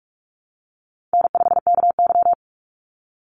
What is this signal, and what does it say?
Morse code, heard as a keyed tone, about 44 wpm, 700 Hz. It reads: N5KY
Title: Easy Call Signs Repeated 3x - 45wpm